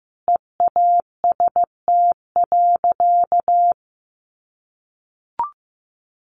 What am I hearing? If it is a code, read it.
Morse code, heard as a keyed tone, about 15 wpm, 700 Hz. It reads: EAST.
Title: Sentences from Top 1200 Words 15wpm